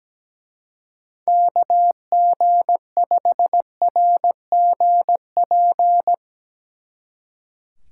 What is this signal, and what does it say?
Morse code, heard as a keyed tone, about 17 wpm, 700 Hz. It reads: KG5RGP
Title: Speed Racing - USA 2x3 Call Signs 17wpm